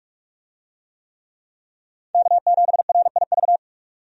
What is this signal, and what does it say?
Morse code, heard as a keyed tone, about 46 wpm, 700 Hz. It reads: K7RIV
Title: USA Call Signs - 5 Characters - 45wpm